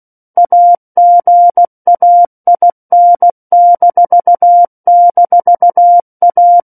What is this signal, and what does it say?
Morse code, heard as a keyed tone, about 16 wpm, 700 Hz. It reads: AGAIN--A
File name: Book One / 07-Chapter 7 - 16 WPM